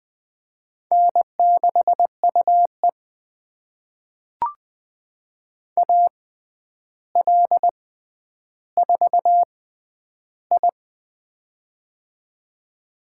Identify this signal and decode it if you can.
Morse code, heard as a keyed tone, about 20 wpm, spaced as for 10 wpm, 700 Hz. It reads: N6UE AL4I
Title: USA Call Signs - 4 Characters - 6x Character Spacing - 20wpm